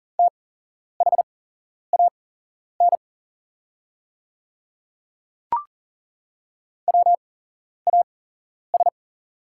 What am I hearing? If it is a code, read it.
Morse code, heard as a keyed tone, about 40 wpm, spaced as for 11 wpm, 690 Hz. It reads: THAN WAS
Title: 8x Character Spacing - Top 100 Words - 40wpm